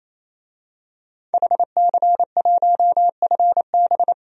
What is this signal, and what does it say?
Morse code, heard as a keyed tone, about 27 wpm, 700 Hz. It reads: HC1FB
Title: DXCC-100 Call Signs - 5 Characters - 28wpm